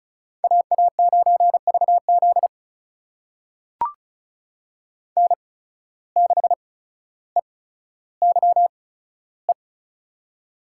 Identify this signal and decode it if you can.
Morse code, heard as a keyed tone, about 36 wpm, spaced as for 15 wpm, 700 Hz. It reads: AA9VZ N6EYE
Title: USA Call Signs - 5 Characters - 8x Character Spacing - 35wpm